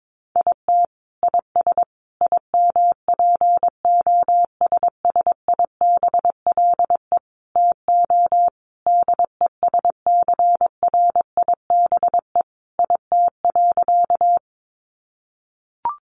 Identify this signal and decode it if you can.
Morse code, heard as a keyed tone, about 22 wpm, 700 Hz. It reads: IT IS IMPOSSIBLE TO DESCRIBE IT.
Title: Sentences from Top 1100 Words 22wpm